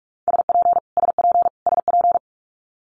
Morse code, heard as a keyed tone, about 46 wpm, 700 Hz. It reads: SP SP SP
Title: Multi-letter Phonemes Repeated 3x 45wpm